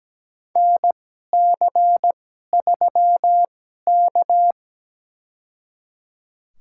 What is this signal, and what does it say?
Morse code, heard as a keyed tone, about 17 wpm, spaced as for 12 wpm, 700 Hz. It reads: NC3K
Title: USA Call Signs - 2x Character Spacing - 17wpm